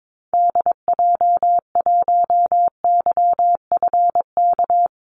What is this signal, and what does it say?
Morse code, heard as a keyed tone, about 22 wpm, 700 Hz. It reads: DJ1YFK
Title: Call Signs Repeated 2x - 22wpm